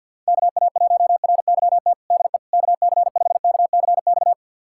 Morse code, heard as a keyed tone, about 51 wpm, 700 Hz. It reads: KA1RYT DE KC5KCX
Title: USA Callsign DE Callsign - 6 Characters - 50wpm